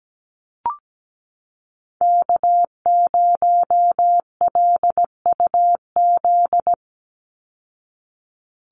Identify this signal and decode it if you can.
Morse code, heard as a keyed tone, about 17 wpm, 700 Hz. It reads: K0LUZ
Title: Call Signs Repeated 2x - 17wpm